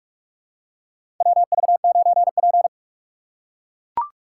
Morse code, heard as a keyed tone, about 41 wpm, 690 Hz. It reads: WV9P
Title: USA Call Signs - 4 Characters - 2x Character Spacing - 45wpm